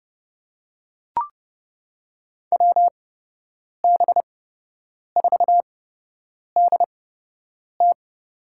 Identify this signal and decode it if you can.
Morse code, heard as a keyed tone, about 30 wpm, spaced as for 8 wpm, 700 Hz. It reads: WB4DT
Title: USA Call Signs - 8x Character Spacing - 30wpm